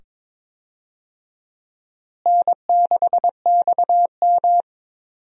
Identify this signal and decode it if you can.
Morse code, heard as a keyed tone, about 22 wpm, 700 Hz. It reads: N6XM
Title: USA Call Signs - 4 Characters - Repeated 2x - 22wpm